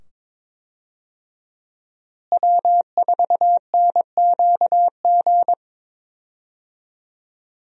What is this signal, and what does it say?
Morse code, heard as a keyed tone, about 22 wpm, 700 Hz. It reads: W4NQG
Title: USA Call Signs - 5 Characters - Repeated 2x - 22wpm